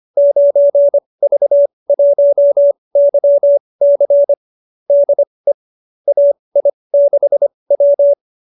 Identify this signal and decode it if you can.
Morse code, heard as a keyed tone, about 25 wpm, spaced as for 20 wpm, 590 Hz. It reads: 9V1YC DE AI6W
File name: qso207b 20